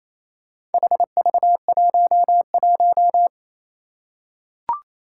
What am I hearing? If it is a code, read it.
Morse code, heard as a keyed tone, about 28 wpm, 700 Hz. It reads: HV11
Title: Four Challenging Letter-Number Combinations 28wpm